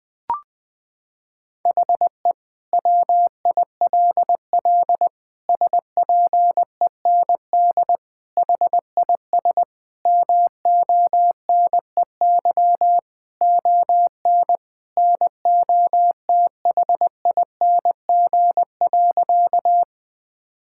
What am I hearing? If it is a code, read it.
Morse code, heard as a keyed tone, about 20 wpm, 700 Hz. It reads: HE WILL SPEND HIS MONEY ON NOTHING.
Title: Sentences from Top 1000 Words - Review Part 5 - 20wpm